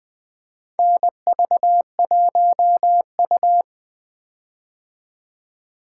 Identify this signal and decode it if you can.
Morse code, heard as a keyed tone, about 20 wpm, 700 Hz. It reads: NV1U
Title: K1USN SST QSO Practice - 20wpm